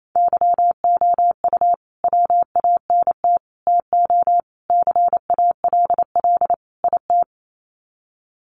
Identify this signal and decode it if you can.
Morse code, heard as a keyed tone, about 28 wpm, 700 Hz. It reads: YOU WANT TO CALL IT
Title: Sets of 5 Words Encore2 28wpm